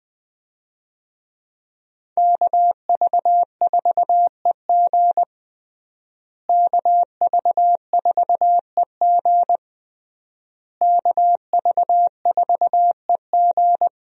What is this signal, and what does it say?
Morse code, heard as a keyed tone, about 21 wpm, 700 Hz. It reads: KV4EG KV4EG KV4EG
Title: USA Call Signs - 5 Characters - Repeated 3x - 20wpm